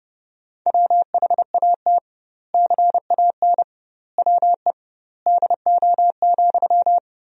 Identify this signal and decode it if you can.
Morse code, heard as a keyed tone, about 30 wpm, 700 Hz. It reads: WHAT CAN WE DO,
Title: 2x Word-Spacing - Sentences from Top 100 Words - 30wpm